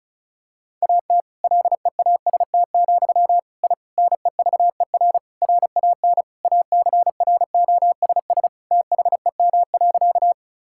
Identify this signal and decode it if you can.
Morse code, heard as a keyed tone, about 34 wpm, 700 Hz. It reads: AT LEAST, I NEVER RAN ACROSS THEM.
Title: Sentences from Top 1000 Words - Review Part 5 - 35wpm